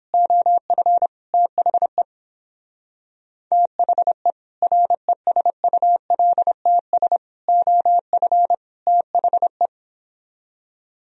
Morse code, heard as a keyed tone, about 26 wpm, 700 Hz. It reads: OF THE THE RESULTS OF THE
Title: Speed-Racing - Sets of 4 Words 22wpm